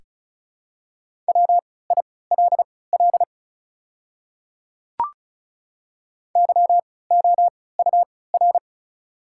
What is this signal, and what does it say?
Morse code, heard as a keyed tone, about 36 wpm, spaced as for 20 wpm, 700 Hz. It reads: WILL YOUR
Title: 3x Character Spacing - Top 200 Words - 35wpm